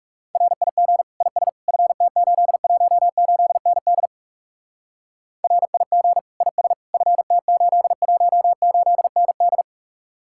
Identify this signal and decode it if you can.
Morse code, heard as a keyed tone, about 37 wpm, 700 Hz. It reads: RIG IS FT818ND RIG IS FT818ND
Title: Speed-Racing Common QSO Phrases Encore - 30wpm